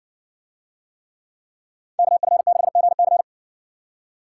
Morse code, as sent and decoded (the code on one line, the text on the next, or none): -.- ..-. -.... --. -.-.
KF6GC